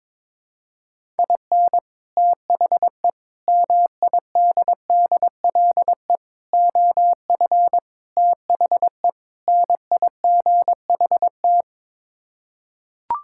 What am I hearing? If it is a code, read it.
Morse code, heard as a keyed tone, about 21 wpm, 700 Hz. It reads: IN THE MIDDLE OF THE NIGHT
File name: practice / 22wpm / Sets of 6 Words - 22wpm